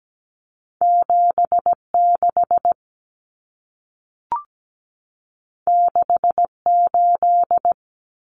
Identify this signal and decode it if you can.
Morse code, heard as a keyed tone, about 17 wpm, 700 Hz. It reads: 76 68
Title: Numbers - 2-Digit 17wpm